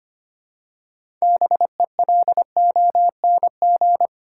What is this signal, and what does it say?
Morse code, heard as a keyed tone, about 25 wpm, 700 Hz. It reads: BELONG
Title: Six Letter Words - 25wpm